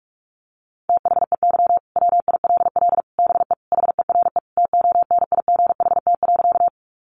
Morse code, heard as a keyed tone, about 44 wpm, 700 Hz. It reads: THEY WILL BE HERE TONIGHT.
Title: Sentences from Top 600 Words 45wpm